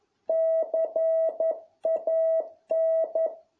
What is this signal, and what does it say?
Morse code, heard as a keyed tone, about 11 wpm, 650 Hz. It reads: CAN